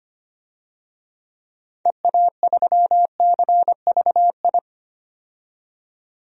Morse code, heard as a keyed tone, about 25 wpm, 700 Hz. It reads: EA3CVI
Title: DXCC-100 Call Signs - 6 Characters - 25wpm